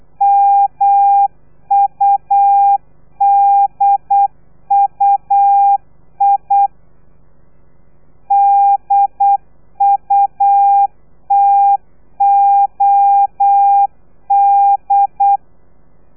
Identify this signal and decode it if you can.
Morse code, heard as a keyed tone, about 8 wpm, 800 Hz. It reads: MUDUI DUTOD
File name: telegrafia / i2tfj / Lez3 35